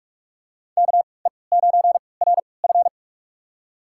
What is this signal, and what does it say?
Morse code, heard as a keyed tone, about 46 wpm, spaced as for 26 wpm, 700 Hz. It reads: KE9RF
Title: USA Call Signs - 5 Characters - 3x Character Spacing - 45wpm